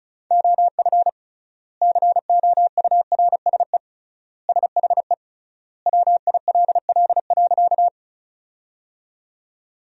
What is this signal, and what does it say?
Morse code, heard as a keyed tone, about 34 wpm, 700 Hz. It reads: OF COURSE SHE WILL.
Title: 3x Word-Spacing - Sentences from Top 500 Words - 35wpm